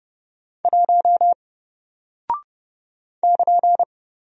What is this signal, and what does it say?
Morse code, heard as a keyed tone, about 31 wpm, 700 Hz. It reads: 1 <KN>
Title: Single Character (Letter:Number:Punctuation:Prosign) 30wpm